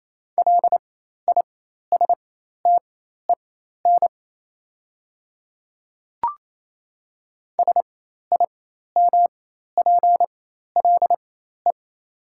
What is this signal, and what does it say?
Morse code, heard as a keyed tone, about 28 wpm, spaced as for 13 wpm, 700 Hz. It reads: LISTEN SIMPLE
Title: 4x Character Spacing - Top 500 Words - 28wpm